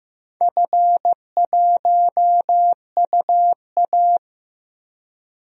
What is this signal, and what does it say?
Morse code, heard as a keyed tone, about 15 wpm, 700 Hz. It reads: F1UA
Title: DXCC-100 Call Signs - 4 Characters - 15wpm